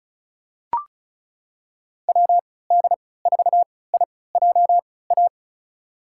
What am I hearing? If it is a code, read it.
Morse code, heard as a keyed tone, about 34 wpm, spaced as for 20 wpm, 700 Hz. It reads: WD4IJA
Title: USA Call Signs - 6 Characters - 3x Character Spacing - 35wpm